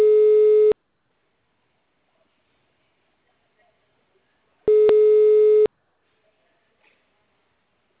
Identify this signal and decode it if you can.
Ringback tone (Europe, 425 Hz).
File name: out- -2004- - - .72700